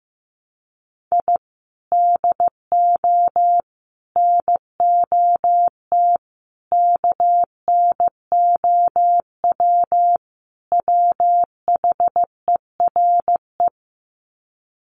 Morse code, heard as a keyed tone, about 15 wpm, 700 Hz. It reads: I DO NOT KNOW WHERE
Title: Sets of 5 Words 15wpm